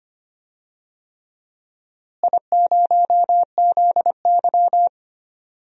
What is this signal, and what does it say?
Morse code, heard as a keyed tone, about 25 wpm, 700 Hz. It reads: I0ZY
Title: DXCC-100 Call Signs - 4 Characters - Repeated 3x - 25wpm